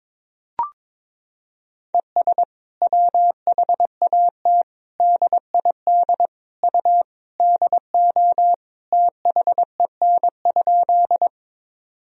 Morse code, heard as a keyed tone, about 22 wpm, 700 Hz. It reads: ES WHAT DID U DO THEN?
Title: Sentences from Top 100 Words and Abbreviations 22wpm